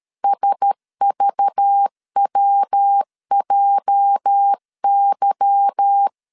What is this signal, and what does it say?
Morse code, heard as a keyed tone, about 13 wpm, 800 Hz. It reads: SVWJY
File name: LEZIONI / 8 YJ45 / 8 YJ 12wpm 2235